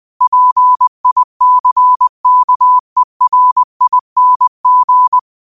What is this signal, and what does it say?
Morse code, heard as a keyed tone, about 20 wpm, 1000 Hz. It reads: PICKERING